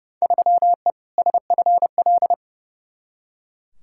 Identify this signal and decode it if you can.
Morse code, heard as a keyed tone, about 30 wpm, 700 Hz. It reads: 3E SFL
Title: Field Day Exchanges Encore - 30wpm